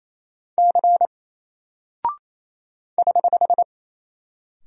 Morse code, heard as a keyed tone, about 27 wpm, 700 Hz. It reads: C <HH>
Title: Single Character (Letter:Number:Punctuation:Prosign) 28wpm